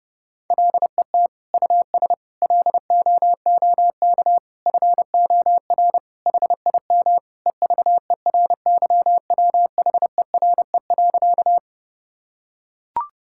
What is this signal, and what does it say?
Morse code, heard as a keyed tone, about 30 wpm, 700 Hz. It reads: LET US LOOK FOR HIM EVERYWHERE.
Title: Sentences from Top 1600 Words 30wpm